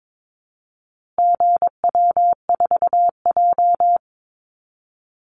Morse code, heard as a keyed tone, about 22 wpm, 700 Hz. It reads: GW4J
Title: Call Signs Repeated 2x - 22wpm